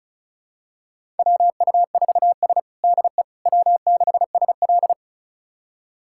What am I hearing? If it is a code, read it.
Morse code, heard as a keyed tone, about 36 wpm, 700 Hz. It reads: WU4S DE W6SL
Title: USA Callsign DE Callsign - 4 Characters - 35wpm